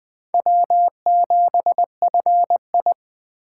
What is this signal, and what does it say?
Morse code, heard as a keyed tone, about 20 wpm, 700 Hz. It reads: W7FI
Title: Speed Racing - USA Call Signs 17wpm